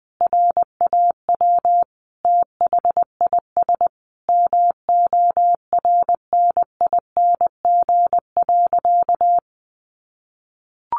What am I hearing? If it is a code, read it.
Morse code, heard as a keyed tone, about 20 wpm, 700 Hz. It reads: RAW THIS MORNING.
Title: Sentences from Top 1500 Words 20wpm